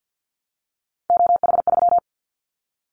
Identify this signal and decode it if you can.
Morse code, heard as a keyed tone, about 50 wpm, 700 Hz. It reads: OH3
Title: Callsign Prefixes for Top 50 DXCC - 50wpm